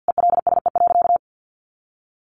Morse code, heard as a keyed tone, about 50 wpm, 700 Hz. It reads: ELSE.
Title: Sentences from Top 800 Words 50wpm